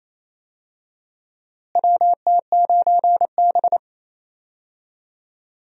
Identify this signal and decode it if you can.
Morse code, heard as a keyed tone, about 28 wpm, 700 Hz. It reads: WT9B
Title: USA Call Signs - 4 Characters - Repeated 3x - 28wpm